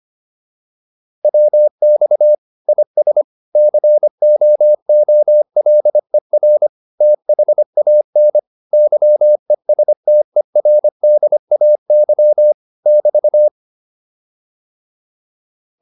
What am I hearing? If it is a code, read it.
Morse code, heard as a keyed tone, about 25 wpm, 600 Hz. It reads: WX IS COOLER THAN YESTERDAY <BT>
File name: QSO409 25